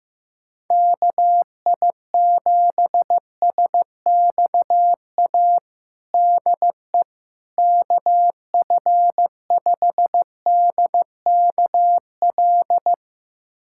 Morse code, heard as a keyed tone, about 15 wpm, 700 Hz. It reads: KI7SXA DE KF5DKL
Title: USA Callsign DE Callsign - 6 Characters - 15wpm